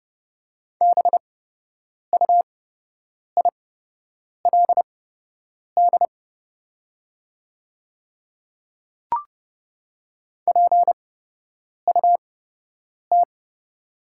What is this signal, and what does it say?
Morse code, heard as a keyed tone, about 30 wpm, spaced as for 8 wpm, 700 Hz. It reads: BUILD PUT
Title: 8x Character Spacing - Top 300 Words - 30wpm